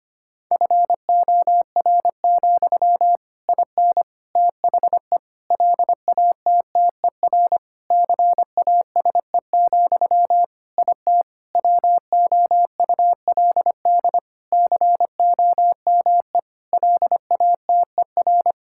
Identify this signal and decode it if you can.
Morse code, heard as a keyed tone, about 25 wpm, 700 Hz. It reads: FOR, IN THE LATTER CASE, IT WOULD COME LATER
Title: Sentences from Top 1600 Words 25wpm